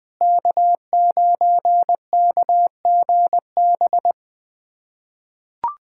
Morse code, heard as a keyed tone, about 20 wpm, 700 Hz. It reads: K9KGB
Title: USA Callsign DE Callsign - 5 Characters - 20wpm